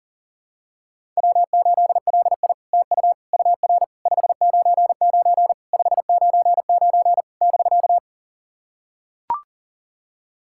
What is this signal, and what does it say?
Morse code, heard as a keyed tone, about 40 wpm, 690 Hz. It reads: W8LI TU UR 599 599 <BK>
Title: SOTA QSO Practice 40wpm